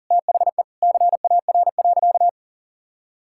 Morse code, heard as a keyed tone, about 40 wpm, 700 Hz. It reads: THE CAR.
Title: Sentences from Top 1000 Words - Review Part 5 - 40wpm